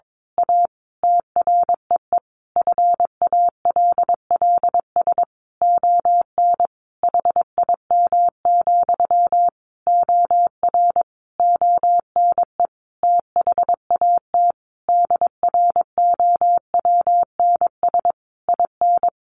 Morse code, heard as a keyed tone, about 22 wpm, 700 Hz. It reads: A TREE FALLS ON HIM, OR ONE THAT DROWNS IN
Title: A Sound Of Thunder 22wpm